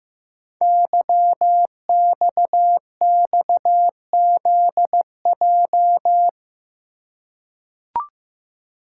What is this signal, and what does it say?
Morse code, heard as a keyed tone, about 15 wpm, 700 Hz. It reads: YXXZJ
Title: Code Groups - 5 Letters - 15wpm